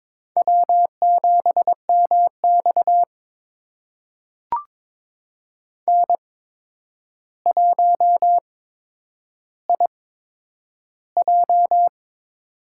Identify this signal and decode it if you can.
Morse code, heard as a keyed tone, about 22 wpm, spaced as for 9 wpm, 700 Hz. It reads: W7MX N1IJ
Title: USA Call Signs - 4 Characters - 8x Character Spacing - 22wpm